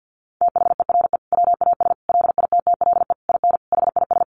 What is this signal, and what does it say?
Morse code, heard as a keyed tone, about 48 wpm, 700 Hz. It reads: THERE WAS LITTLE IN HIS
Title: Sentences from Top 1400 Words 50wpm